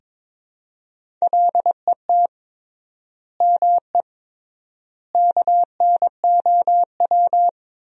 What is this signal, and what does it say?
Morse code, heard as a keyed tone, about 23 wpm, 700 Hz. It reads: LET ME KNOW
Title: 3x Word-Spacing - Sentences from Top 300 Words - 22wpm